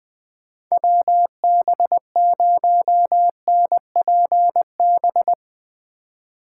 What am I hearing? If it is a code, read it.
Morse code, heard as a keyed tone, about 20 wpm, 700 Hz. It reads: WB0NPB
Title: USA Call Signs - 6 Characters - 2x Character Spacing - 20wpm